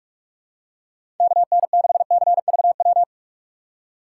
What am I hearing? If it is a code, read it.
Morse code, heard as a keyed tone, about 44 wpm, 700 Hz. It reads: KN6CVW
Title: USA Call Signs - 3x Character Spacing - 45wpm